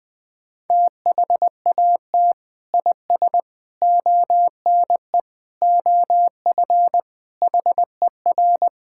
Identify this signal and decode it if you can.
Morse code, heard as a keyed tone, about 20 wpm, 700 Hz. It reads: THAT IS ONE OF HER
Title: Sentences with 2-Word Phrases - 20wpm